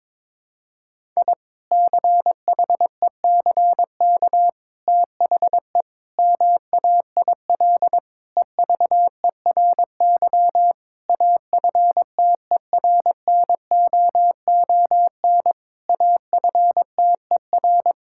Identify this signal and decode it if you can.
Morse code, heard as a keyed tone, about 22 wpm, 700 Hz. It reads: I CHECK THE MAIL EVERY AFTERNOON AFTER
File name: Difficult Sentences with Hints - 22wpm